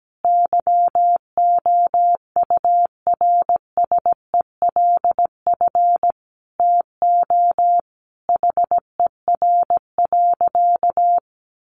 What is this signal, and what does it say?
Morse code, heard as a keyed tone, about 17 wpm, 700 Hz. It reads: YOURSELF TO HER.
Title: Sentences from Top 1100 Words 17wpm